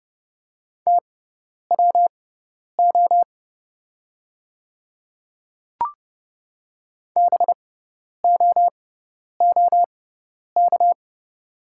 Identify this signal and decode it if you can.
Morse code, heard as a keyed tone, about 29 wpm, spaced as for 10 wpm, 690 Hz. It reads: TWO BOOK
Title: 6x Character Spacing - Top 300 Words - 30wpm